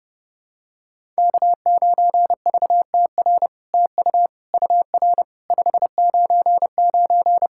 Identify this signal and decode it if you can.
Morse code, heard as a keyed tone, about 30 wpm, 700 Hz. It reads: K9VTR TU UR 599
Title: SOTA QSO Practice 30wpm